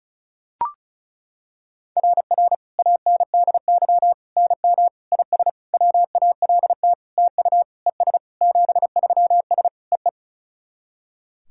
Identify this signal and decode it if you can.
Morse code, heard as a keyed tone, about 36 wpm, 700 Hz. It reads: RR ANDY NM IS WALT TU ES 73S EE
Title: SOTA QSO Practice 35wpm